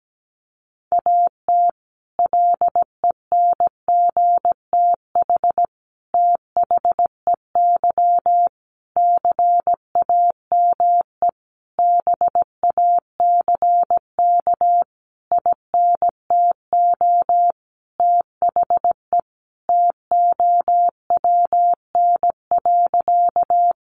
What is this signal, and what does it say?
Morse code, heard as a keyed tone, about 17 wpm, 700 Hz. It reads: AT LENGTH THEY CAME BACK INTO THE TOWN.
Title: Sentences from Top 1000 Words 17wpm